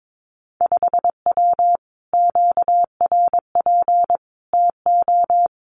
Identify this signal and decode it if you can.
Morse code, heard as a keyed tone, about 22 wpm, 700 Hz. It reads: 5W QRP TO
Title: Common QSO Phrases Encore 22wpm